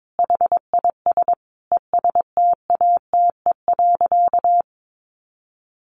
Morse code, heard as a keyed tone, about 22 wpm, 700 Hz. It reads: HIS ESTATE.
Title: Sentences from Top 1200 Words 22wpm